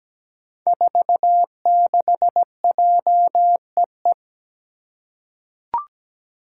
Morse code, heard as a keyed tone, about 17 wpm, 700 Hz. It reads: 46JEE
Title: Five Challenging Letter-Number Combinations 17wpm